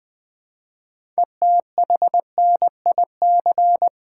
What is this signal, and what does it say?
Morse code, heard as a keyed tone, about 20 wpm, 700 Hz. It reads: ETHNIC
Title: 6 Letter Words - First 3 Letters Given - 20wpm